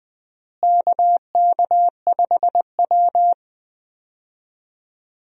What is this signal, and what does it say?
Morse code, heard as a keyed tone, about 20 wpm, 700 Hz. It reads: KK5W
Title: USA Call Signs - 4 Characters - Repeated 2x - 20wpm